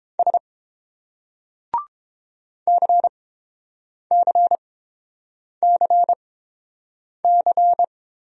Morse code, heard as a keyed tone, about 28 wpm, spaced as for 7 wpm, 690 Hz. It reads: S CCCC